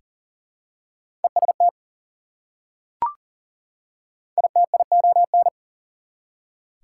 Morse code, heard as a keyed tone, about 40 wpm, 700 Hz. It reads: EST ITION
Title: Prefixes and Suffixes 40wpm